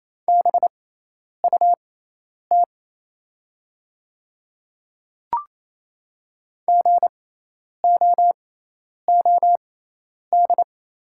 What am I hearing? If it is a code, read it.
Morse code, heard as a keyed tone, about 27 wpm, spaced as for 10 wpm, 700 Hz. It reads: BUT GOOD
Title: 6x Character Spacing - Top 300 Words - 28wpm